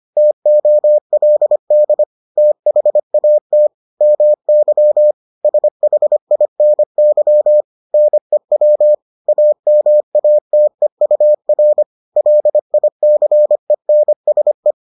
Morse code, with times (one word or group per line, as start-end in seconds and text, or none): 0.2-2.0 TOLD
2.4-3.7 THAT
4.0-5.1 MY
5.4-7.6 SHINY
7.9-8.9 NEW
9.3-11.8 AMATEUR
12.2-14.7 LICENSE